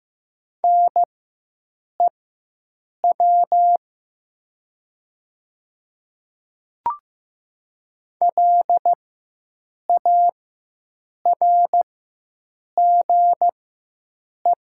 Morse code, heard as a keyed tone, about 15 wpm, spaced as for 7 wpm, 700 Hz. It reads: NEW LARGE
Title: 4x Character Spacing - Top 200 Words - 15wpm